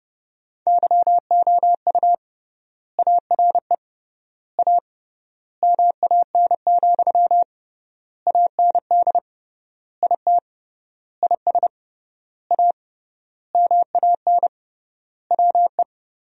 Morse code, heard as a keyed tone, about 29 wpm, 700 Hz. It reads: YOU ARE A MAN, AND IT IS A MAN WE
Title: 3x Word-Spacing - Sentences from Top 200 Words - 30wpm